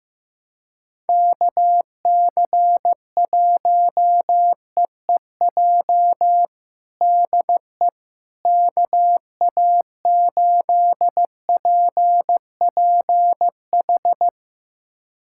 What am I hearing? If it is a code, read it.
Morse code, heard as a keyed tone, about 15 wpm, 700 Hz. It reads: KC1EEJ DE KA8PPH
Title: USA Callsign DE Callsign - 6 Characters - 15wpm